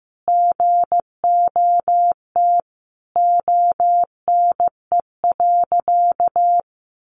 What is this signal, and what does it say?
Morse code, heard as a keyed tone, about 15 wpm, 700 Hz. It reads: GOT ONE.
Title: Sentences from Top 300 Words 15wpm